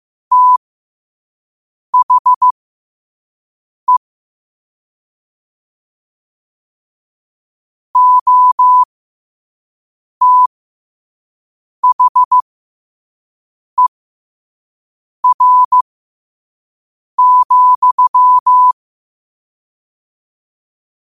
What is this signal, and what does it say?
Morse code, heard as a keyed tone, about 15 wpm, spaced as for 5 wpm, 1000 Hz. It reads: THE OTHER,
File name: qotdfiles / 5wpm